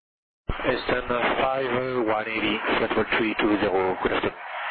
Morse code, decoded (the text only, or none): none